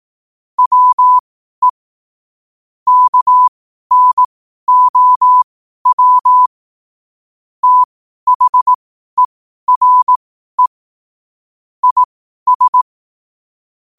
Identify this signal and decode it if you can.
Morse code, heard as a keyed tone, about 18 wpm, spaced as for 12 wpm, 1000 Hz. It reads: WE KNOW THERE IS
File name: qotdfiles / 12wpm